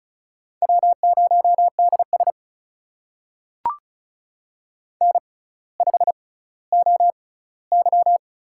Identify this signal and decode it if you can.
Morse code, heard as a keyed tone, about 36 wpm, spaced as for 18 wpm, 700 Hz. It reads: W0DS N5OY
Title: USA Call Signs - 4 Characters - 6x Character Spacing - 35wpm